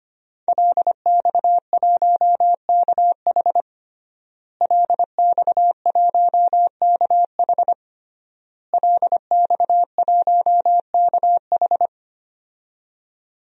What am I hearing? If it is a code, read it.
Morse code, heard as a keyed tone, about 25 wpm, 700 Hz. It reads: LX1KH LX1KH LX1KH
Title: DXCC-100 Call Signs - 5 Characters - Repeated 3x - 25wpm